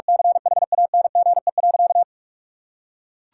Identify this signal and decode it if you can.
Morse code, heard as a keyed tone, about 46 wpm, 700 Hz. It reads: CHANGE.
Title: Sentences from Top 1000 Words - Review Part 6 - 45wpm